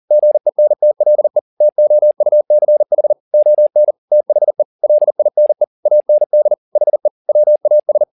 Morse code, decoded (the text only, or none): GENTLE TOUCH ON THE LINE AND HE WAS